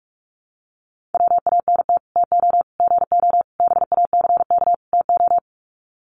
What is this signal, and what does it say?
Morse code, heard as a keyed tone, about 44 wpm, 700 Hz. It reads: WANT TO GO BACK TO
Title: Sets of 5 Words 45wpm